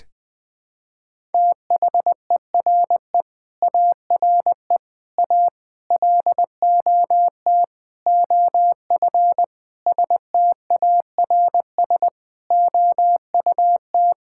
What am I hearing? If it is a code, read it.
Morse code, heard as a keyed tone, about 20 wpm, 700 Hz. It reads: THERE ARE A LOT OF STARS OUT
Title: Sentences with 5-Word Phrases - Spoken Beginning - 20wpm